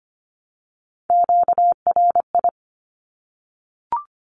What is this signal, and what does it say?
Morse code, heard as a keyed tone, about 25 wpm, 700 Hz. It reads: QRI
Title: Q-Codes 25wpm